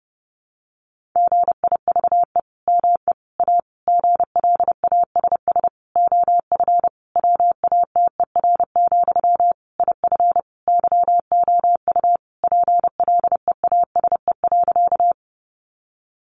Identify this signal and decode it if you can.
Morse code, heard as a keyed tone, about 30 wpm, 700 Hz. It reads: GIVE ME A GLASS OF WATER, IF YOU PLEASE.
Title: Sentences from Top 1500 Words 30wpm